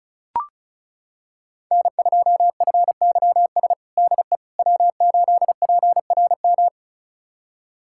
Morse code, heard as a keyed tone, about 36 wpm, 700 Hz. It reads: N2FYS DE W8PRM
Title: USA Callsign DE Callsign - 5 Characters - 35wpm